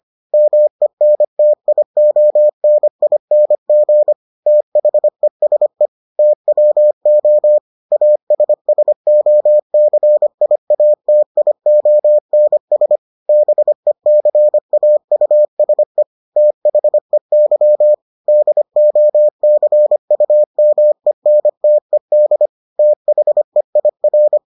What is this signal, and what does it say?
Morse code, heard as a keyed tone, about 25 wpm, 600 Hz. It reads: MENTIONING THESE TWO ASSOCIATIONS BECAUSE THEY DOCUMENTED THEIR